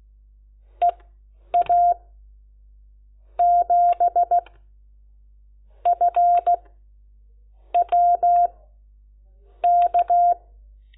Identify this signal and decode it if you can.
Morse code, heard as a keyed tone, about 16 wpm, spaced as for 6 wpm, 700 Hz. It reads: EA7FWK
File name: Lecciones / CLASE Nº10